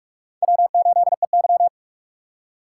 Morse code, heard as a keyed tone, about 44 wpm, 700 Hz. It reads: W8EY
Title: USA Call Signs - 4 Characters - 6x Character Spacing - 45wpm